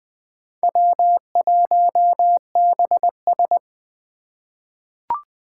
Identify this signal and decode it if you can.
Morse code, heard as a keyed tone, about 20 wpm, 700 Hz. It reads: W1BS
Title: Four Challenging Letter-Number Combinations 20wpm